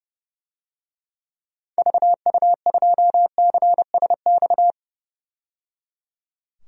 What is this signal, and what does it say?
Morse code, heard as a keyed tone, about 30 wpm, 700 Hz. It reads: VU2CSX
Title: DXCC-100 Call Signs - 6 Characters - 30wpm